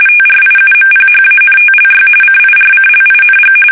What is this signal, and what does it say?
RTTY, about 45.45 baud. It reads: TRIUMPHANTLY CIRCLING
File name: fsk850 / battleshipnj